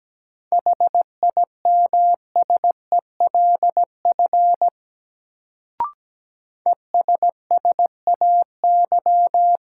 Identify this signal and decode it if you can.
Morse code, heard as a keyed tone, about 17 wpm, 700 Hz. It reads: HIMSELF ESSAY
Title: Sentences from Top 1500 Words - New Word Drill - 17wpm